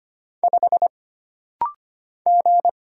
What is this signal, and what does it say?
Morse code, heard as a keyed tone, about 25 wpm, 700 Hz. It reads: 5 G
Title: Single Letter-Number Alt - Rapid-Fire 25wpm